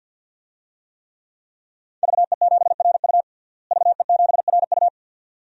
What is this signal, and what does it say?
Morse code, heard as a keyed tone, about 52 wpm, 700 Hz. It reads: VE7RU VE7RU
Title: DXCC-100 Call Signs - 5 Characters - Repeated 2x - 50wpm